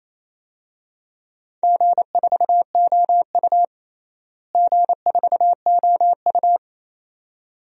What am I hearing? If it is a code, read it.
Morse code, heard as a keyed tone, about 28 wpm, 700 Hz. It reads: G4OU G4OU
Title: DXCC-100 Call Signs - 4 Characters - Repeated 2x - 28wpm